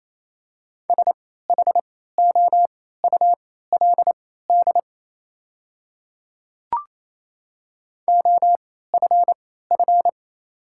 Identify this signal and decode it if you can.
Morse code, heard as a keyed tone, about 28 wpm, spaced as for 16 wpm, 690 Hz. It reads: SHOULD OFF
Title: 3x Character Spacing - Top 500 Words - 28wpm